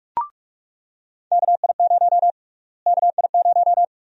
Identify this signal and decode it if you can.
Morse code, heard as a keyed tone, about 45 wpm, 700 Hz. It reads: KI0 KI0
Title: USA Call Sign Prefixes - Repeated 2x - 45wpm